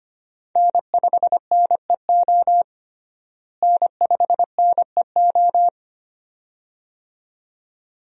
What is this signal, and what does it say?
Morse code, heard as a keyed tone, about 25 wpm, 700 Hz. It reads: N5NEO N5NEO
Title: USA Call Signs - 5 Characters - Repeated 2x - 25wpm